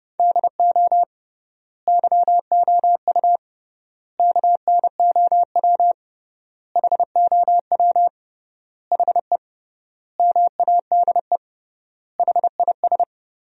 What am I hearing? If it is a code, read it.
Morse code, heard as a keyed tone, about 32 wpm, 700 Hz. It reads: DO YOU KNOW HOW HE MADE HIS
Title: 3x Word-Spacing - Sentences from Top 300 Words - 30wpm